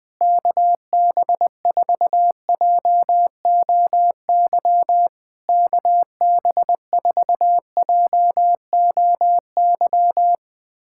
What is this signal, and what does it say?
Morse code, heard as a keyed tone, about 20 wpm, 700 Hz. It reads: KB4JOY KB4JOY
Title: POTA QSO Practice - 20wpm